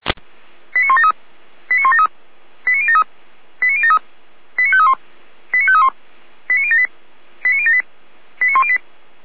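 Five-tone selective call (ZVEI), heard as quick runs of five tones, tones of about 70 ms. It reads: 89173 89173 89073 89073 89531 89531 89078 89078 89108
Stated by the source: Five-tone ZVEI selective call 89018 received by SDR.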